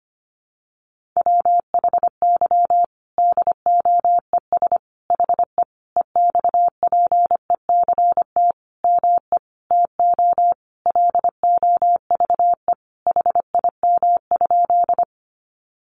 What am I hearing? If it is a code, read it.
Morse code, heard as a keyed tone, about 25 wpm, 700 Hz. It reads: WHY DOES HE EXPECT ME TO LOVE HIM?
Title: Sentences from Top 1100 Words 25wpm